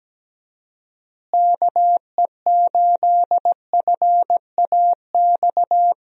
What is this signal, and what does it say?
Morse code, heard as a keyed tone, about 17 wpm, 700 Hz. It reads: KE8FAX
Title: USA Call Signs - 6 Characters - Repeated 3x - 17wpm